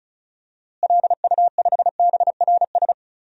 Morse code, heard as a keyed tone, about 36 wpm, 700 Hz. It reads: LU5BRS
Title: DXCC-100 Call Signs - 6 Characters - Repeated 3x - 35wpm